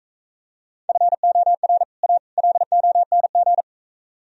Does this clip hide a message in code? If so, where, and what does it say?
Yes: Morse code from 0.9 to 3.6 s, heard as a keyed tone, about 43 wpm, 700 Hz. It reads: FOR A LONG